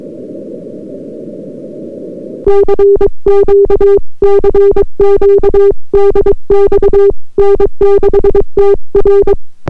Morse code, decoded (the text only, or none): CQ CQ DX N6TR